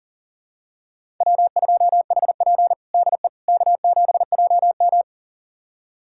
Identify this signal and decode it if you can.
Morse code, heard as a keyed tone, about 40 wpm, 690 Hz. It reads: W2HP DE K7JM
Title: USA Callsign DE Callsign - 4 Characters - 40wpm